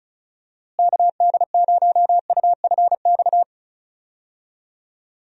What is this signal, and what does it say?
Morse code, heard as a keyed tone, about 34 wpm, 700 Hz. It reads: KD0UFX
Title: USA Call Signs - 6 Characters - Repeated 2x - 35wpm